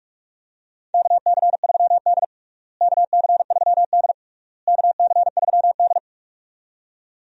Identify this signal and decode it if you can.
Morse code, heard as a keyed tone, about 41 wpm, 700 Hz. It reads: KC3D KC3D KC3D
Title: USA Call Signs - 4 Characters - Repeated 3x - 45wpm